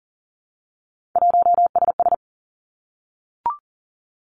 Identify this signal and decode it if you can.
Morse code, heard as a keyed tone, about 41 wpm, 700 Hz. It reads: 1SS
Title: Three Challenging Letter-Number Combinations 40wpm